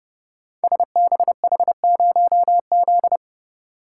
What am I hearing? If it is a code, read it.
Morse code, heard as a keyed tone, about 31 wpm, 700 Hz. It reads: SBH0Z
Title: Code Groups - 5 Letter:Numbers - 30wpm